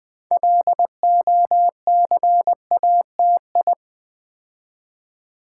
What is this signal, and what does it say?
Morse code, heard as a keyed tone, about 21 wpm, 700 Hz. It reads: LOCATI
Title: Build A Word - 3 Letters At A Time - 8-Letter Words - 20wpm